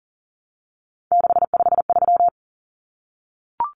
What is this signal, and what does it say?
Morse code, heard as a keyed tone, about 40 wpm, 700 Hz. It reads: 653
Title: Numbers - 3-Digit 40wpm